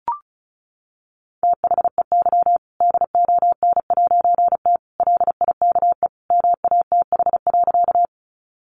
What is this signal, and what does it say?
Morse code, heard as a keyed tone, about 34 wpm, 700 Hz. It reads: THEY DON'T LIKE MATH.
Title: Easy Phrases 35wpm